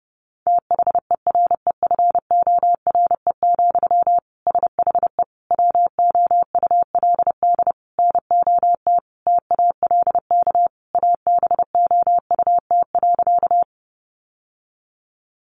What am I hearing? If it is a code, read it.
Morse code, heard as a keyed tone, about 30 wpm, 700 Hz. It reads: THEREFORE, SHE WOULD NOT TALK ABOUT.
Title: Sentences from Top 800 Words 30wpm